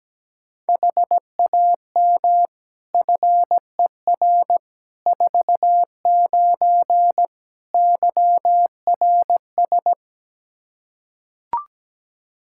Morse code, heard as a keyed tone, about 17 wpm, 700 Hz. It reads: HAM FER 49 YRS
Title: K1USN SST QSO Practice 17wpm